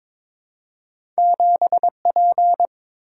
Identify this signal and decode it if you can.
Morse code, heard as a keyed tone, about 22 wpm, 700 Hz. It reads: 7P
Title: Single-Letter Single-Number 22wpm